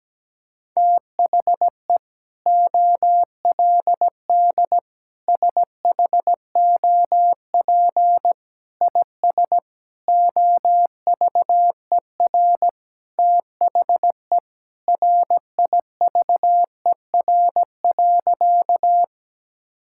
Morse code, heard as a keyed tone, about 17 wpm, 700 Hz. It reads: THE OLD SHOP IS OVER THE RIVER.
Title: Sentences from Top 1600 Words 17wpm